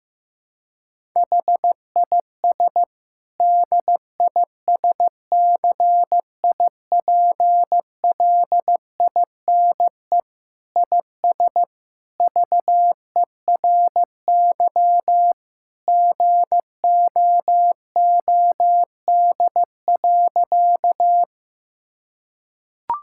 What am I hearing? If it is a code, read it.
Morse code, heard as a keyed tone, about 15 wpm, 700 Hz. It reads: HIS DISCIPLINE IS VERY GOOD.
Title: Sentences from Top 1000 Words - Review Part 6 - 15wpm